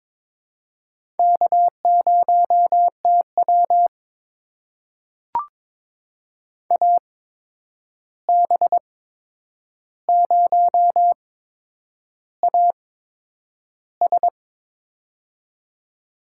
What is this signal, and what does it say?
Morse code, heard as a keyed tone, about 22 wpm, spaced as for 9 wpm, 700 Hz. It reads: K0TW AB0AS
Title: USA Call Signs - 8x Character Spacing - 22wpm